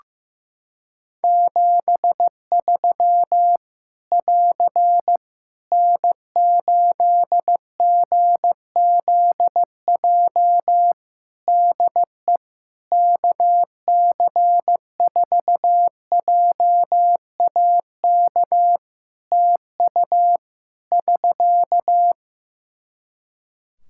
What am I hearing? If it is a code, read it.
Morse code, heard as a keyed tone, about 15 wpm, 700 Hz. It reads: 73 <AR> N8GZJ DE KC4JAK TU <SK>
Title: Practice QSO - Exchange 4 - Ending - 15wpm